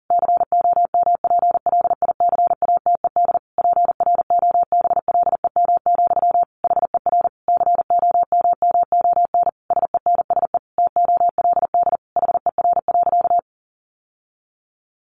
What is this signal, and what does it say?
Morse code, heard as a keyed tone, about 40 wpm, 700 Hz. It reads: COMPLICATED PROBLEM, HER COMMON SENSE TOLD HER.
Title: Sentences from Top 1200 Words 40wpm